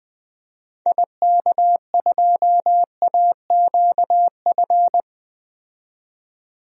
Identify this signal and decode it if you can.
Morse code, heard as a keyed tone, about 20 wpm, 700 Hz. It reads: IK2AQF
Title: DXCC-100 Call Signs - 6 Characters - 20wpm